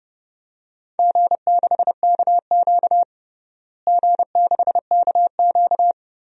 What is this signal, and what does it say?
Morse code, heard as a keyed tone, about 29 wpm, 700 Hz. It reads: G6KQ G6KQ
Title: DXCC-100 Call Signs - 4 Characters - Repeated 3x - 30wpm